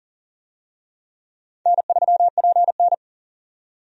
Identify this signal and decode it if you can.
Morse code, heard as a keyed tone, about 40 wpm, 700 Hz. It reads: N3PN
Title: USA Call Signs - 4 Characters - 40wpm